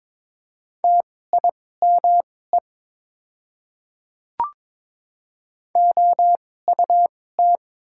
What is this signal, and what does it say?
Morse code, heard as a keyed tone, about 22 wpm, spaced as for 16 wpm, 700 Hz. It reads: TIME OUT
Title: 2x Character Spacing - Top 300 Words - 22wpm